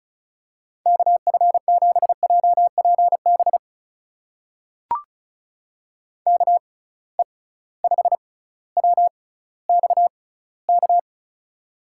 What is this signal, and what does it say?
Morse code, heard as a keyed tone, about 36 wpm, spaced as for 18 wpm, 700 Hz. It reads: KF7JPB KE5WXK
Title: USA Call Signs - 6 Characters - 6x Character Spacing - 35wpm